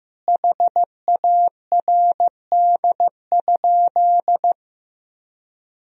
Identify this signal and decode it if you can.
Morse code, heard as a keyed tone, about 16 wpm, 700 Hz. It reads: HARD?
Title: Sentences from Top 1000 Words - Review Part 4 - 15wpm